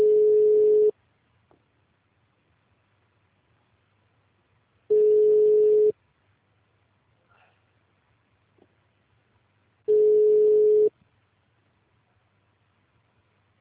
Ringback tone (Europe, 425 Hz).
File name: out- -2020- - - .20